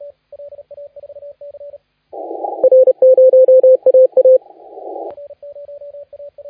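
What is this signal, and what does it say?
Morse code, heard as a keyed tone, about 30 wpm, 520 Hz, with radio noise. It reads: R0AA
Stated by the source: Morse contest audio recorded by LA4C during the SAC CW 2018 contest.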